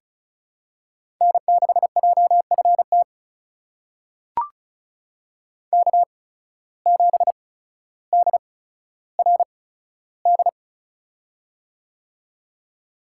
Morse code, heard as a keyed tone, about 36 wpm, spaced as for 15 wpm, 700 Hz. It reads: N6JFT K7DRD
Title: USA Call Signs - 5 Characters - 8x Character Spacing - 35wpm